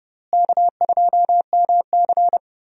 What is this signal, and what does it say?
Morse code, heard as a keyed tone, about 32 wpm, 700 Hz. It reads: K2MC
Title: USA Call Signs - 4 Characters - Repeated 2x - 30wpm